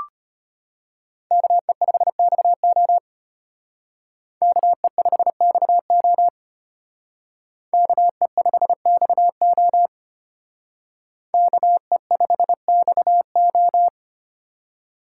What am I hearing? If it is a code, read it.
Morse code, heard as a keyed tone, about 32 wpm, 700 Hz. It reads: KE5XO KE5XO KE5XO KE5XO